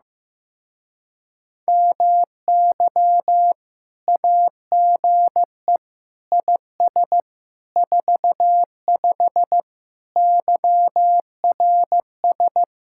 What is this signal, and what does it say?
Morse code, heard as a keyed tone, about 16 wpm, 700 Hz. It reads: MY AGE IS 45 YRS
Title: Speed-Racing Common QSO Phrases Encore - 15wpm